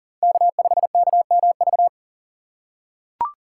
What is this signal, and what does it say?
Morse code, heard as a keyed tone, about 37 wpm, 700 Hz. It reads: K5KMV
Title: USA Call Signs - 5 Characters - 4x Character Spacing - 40wpm